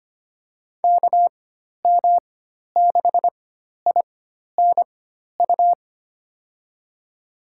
Morse code, heard as a keyed tone, about 25 wpm, spaced as for 12 wpm, 700 Hz. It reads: KM6INU
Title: USA Call Signs - 6 Characters - 4x Character Spacing - 25wpm